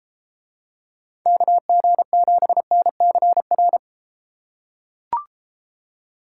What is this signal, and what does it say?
Morse code, heard as a keyed tone, about 33 wpm, 700 Hz. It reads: KG7NCR